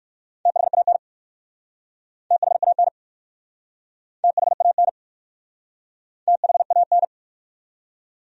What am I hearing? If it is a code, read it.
Morse code, heard as a keyed tone, about 59 wpm, 690 Hz. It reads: THAN THAN THAN THAN